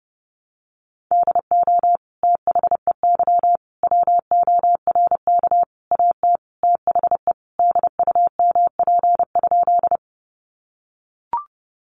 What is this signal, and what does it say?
Morse code, heard as a keyed tone, about 30 wpm, 700 Hz. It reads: DO THEY WORK AT THE DUMP?
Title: Sentences from Top 1300 Words 30wpm